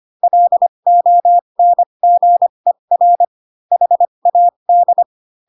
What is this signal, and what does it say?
Morse code, heard as a keyed tone, about 25 wpm, spaced as for 22 wpm, 700 Hz. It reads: LONGER HAD
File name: Pinocchio / Pinocchio ch6 18 25 ik1hge